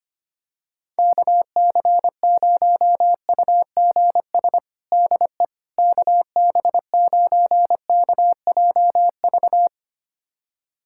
Morse code, heard as a keyed tone, about 25 wpm, 700 Hz. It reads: KC0UGS DE KB9KJV
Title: USA Callsign DE Callsign - 6 Characters - 25wpm